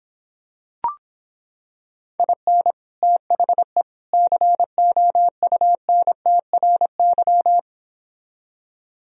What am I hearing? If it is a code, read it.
Morse code, heard as a keyed tone, about 26 wpm, 700 Hz. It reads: IN THE COUNTRY